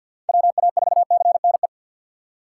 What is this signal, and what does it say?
Morse code, heard as a keyed tone, about 47 wpm, 700 Hz. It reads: WA4CNE
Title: USA Call Signs - 3x Character Spacing - 50wpm